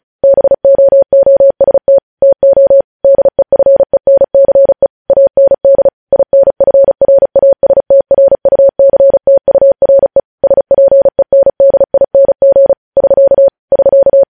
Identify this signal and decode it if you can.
Morse code, heard as a keyed tone, about 36 wpm, 550 Hz. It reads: BOOST TO DEFENCE AND INFRASTRUCTURE SPENDING <SK> <SK>
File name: podcast / News headlines-1x-35x35